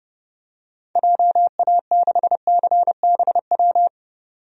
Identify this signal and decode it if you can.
Morse code, heard as a keyed tone, about 32 wpm, 700 Hz. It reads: JA6CBW
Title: DXCC-100 Call Signs - 6 Characters - 30wpm